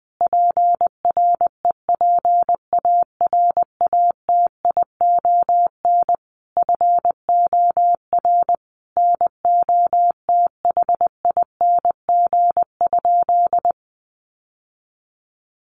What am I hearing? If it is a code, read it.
Morse code, heard as a keyed tone, about 20 wpm, 700 Hz. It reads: PREPARATION FOR NOTHING?
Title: Sentences from Top 1400 Words 20wpm